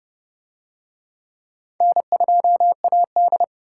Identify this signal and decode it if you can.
Morse code, heard as a keyed tone, about 29 wpm, 700 Hz. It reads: N2AD